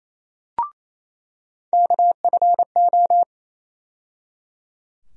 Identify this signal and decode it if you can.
Morse code, heard as a keyed tone, about 28 wpm, 700 Hz. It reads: KFO
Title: USA Call Sign Postfixes - 3 Letters 28wpm